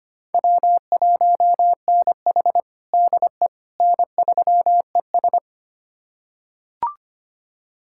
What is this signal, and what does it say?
Morse code, heard as a keyed tone, about 25 wpm, 700 Hz. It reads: W1NH DE N3ES
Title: USA Callsign DE Callsign - 4 Characters - 25wpm